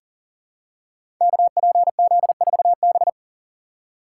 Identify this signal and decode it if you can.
Morse code, heard as a keyed tone, about 38 wpm, 700 Hz. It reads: KPZ4B
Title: Code Groups - 5 Letter:Numbers - 40wpm